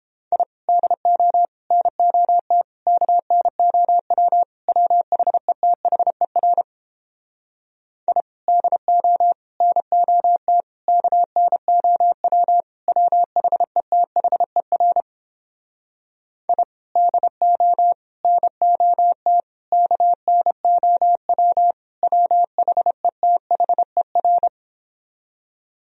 Morse code, heard as a keyed tone, about 29 wpm, 700 Hz. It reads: I DO NOT KNOW WHETHER I DO NOT KNOW WHETHER I DO NOT KNOW WHETHER